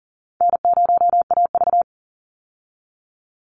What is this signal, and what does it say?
Morse code, heard as a keyed tone, about 40 wpm, 700 Hz. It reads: N0AV
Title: Easy Call Signs Repeated 3x - 40wpm